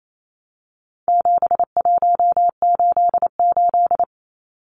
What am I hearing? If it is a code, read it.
Morse code, heard as a keyed tone, about 28 wpm, 700 Hz. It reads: 7188
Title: Numbers - 4-Digit 28wpm